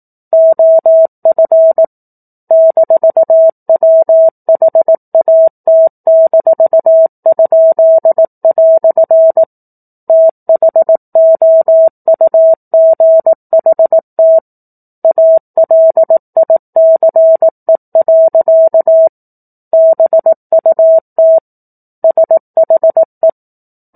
Morse code, heard as a keyed tone, about 18 wpm, 650 Hz. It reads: OF -WHAT-?" THOUGHT ALICE. BUT SHE
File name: 04-Chapter-04 18 WPM